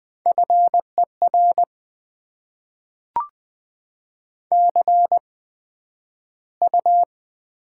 Morse code, heard as a keyed tone, about 20 wpm, spaced as for 11 wpm, 700 Hz. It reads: FER CU
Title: Top 20 QSO Elements - 8x Character Spacing - 20wpm